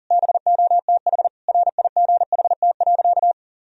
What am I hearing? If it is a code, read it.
Morse code, heard as a keyed tone, about 40 wpm, 700 Hz. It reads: BOTH RIGHT.
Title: Sentences from Top 1000 Words - Review Part 1 - 40wpm